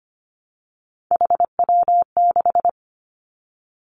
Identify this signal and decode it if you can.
Morse code, heard as a keyed tone, about 25 wpm, 700 Hz. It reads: HW6
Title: Three Challenging Letter-Number Combinations 25wpm